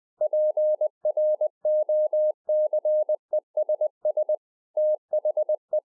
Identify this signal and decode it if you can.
Morse code, heard as a keyed tone, about 20 wpm, 610 Hz. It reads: PROCESS THE